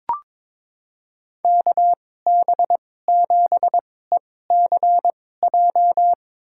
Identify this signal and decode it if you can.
Morse code, heard as a keyed tone, about 22 wpm, spaced as for 16 wpm, 700 Hz. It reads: KB7ECJ
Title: USA Call Signs - 6 Characters - 2x Character Spacing - 22wpm